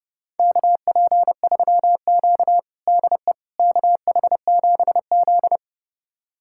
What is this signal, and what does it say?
Morse code, heard as a keyed tone, about 31 wpm, 700 Hz. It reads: KP3Q DE KH7Z
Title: USA Callsign DE Callsign - 4 Characters - 30wpm